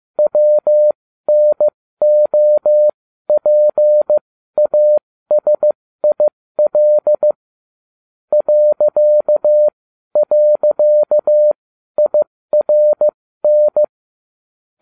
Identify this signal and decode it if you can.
Morse code, heard as a keyed tone, about 15 wpm, spaced as for 12 wpm, 600 Hz. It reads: WNOPASIL ..IRN